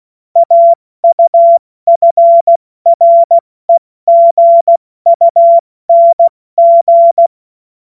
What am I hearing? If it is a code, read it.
Morse code, heard as a keyed tone, about 16 wpm, 680 Hz. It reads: AUFREGUNG
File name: Morsekurs / Witze / Witz 02. August 2022 70 F80